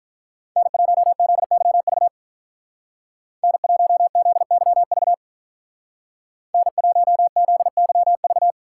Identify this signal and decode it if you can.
Morse code, heard as a keyed tone, about 43 wpm, 700 Hz. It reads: N1ZYV N1ZYV N1ZYV
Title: Speed Racing - USA Call Signs 35wpm